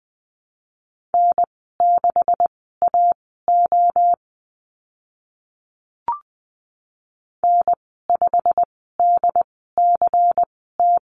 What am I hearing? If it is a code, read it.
Morse code, heard as a keyed tone, about 20 wpm, spaced as for 15 wpm, 700 Hz. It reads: N6AO N5DCT
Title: 161 Lesson 26g - Letter g - Call Signs 20wpm